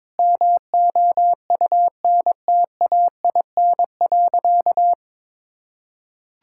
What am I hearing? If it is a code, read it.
Morse code, heard as a keyed tone, about 22 wpm, 700 Hz. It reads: MOUNTAIN.
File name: practice / 22wpm / Sentences with 3-Word Phrases - 22wpm